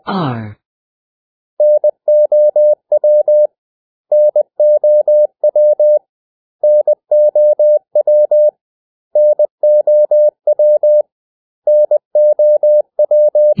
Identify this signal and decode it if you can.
Morse code, heard as a keyed tone, about 20 wpm, 600 Hz. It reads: NOW NOW NOW NOW NOW
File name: cw / 100 Common English Words